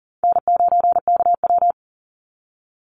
Morse code, heard as a keyed tone, about 40 wpm, 700 Hz. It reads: N9KW
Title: NA SOTA Chaser Callsigns 40wpm